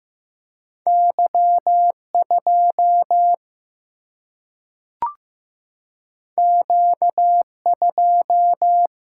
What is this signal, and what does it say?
Morse code, heard as a keyed tone, about 15 wpm, 700 Hz. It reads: Y2 Q2
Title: Single-Letter Single-Number 15wpm